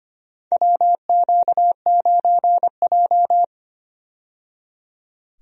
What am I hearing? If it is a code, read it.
Morse code, heard as a keyed tone, about 25 wpm, 700 Hz. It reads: WQ9J
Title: USA Call Signs - 4 Characters - 25wpm